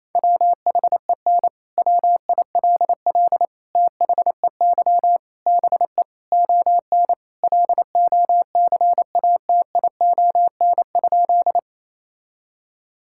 Morse code, heard as a keyed tone, about 28 wpm, 700 Hz. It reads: WHEN WILL THEY BE ON LOCATION?
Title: Sentences from Top 1000 Words - Review Part 6 - 28wpm